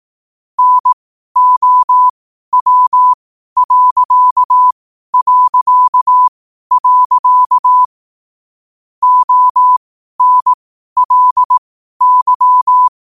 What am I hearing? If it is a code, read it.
Morse code, heard as a keyed tone, about 18 wpm, spaced as for 12 wpm, 1000 Hz. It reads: NOW... ONLY
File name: qotdfiles / 12wpm